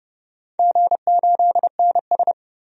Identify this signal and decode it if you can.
Morse code, heard as a keyed tone, about 30 wpm, 700 Hz. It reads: G8NS
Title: DXCC-100 Call Signs - 4 Characters - 30wpm